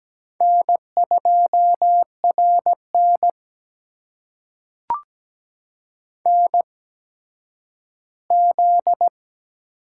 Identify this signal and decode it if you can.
Morse code, heard as a keyed tone, about 17 wpm, spaced as for 10 wpm, 700 Hz. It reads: N2RN NZ
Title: USA Call Signs - 4 Characters - 8x Character Spacing - 17wpm